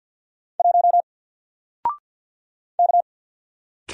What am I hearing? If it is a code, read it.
Morse code, heard as a keyed tone, about 48 wpm, 700 Hz. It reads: 1 K
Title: Single Character (Letter:Number:Punctuation:Prosign) Alt 50wpm